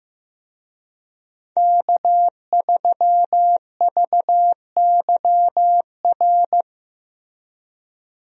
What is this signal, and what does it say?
Morse code, heard as a keyed tone, about 15 wpm, 700 Hz. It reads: K3VYR
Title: USA Call Signs - 5 Characters - Repeated 3x - 15wpm